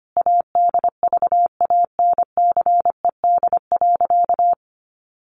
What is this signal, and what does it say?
Morse code, heard as a keyed tone, about 25 wpm, 700 Hz. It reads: ADVANCED.
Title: Sentences from Top 1200 Words 25wpm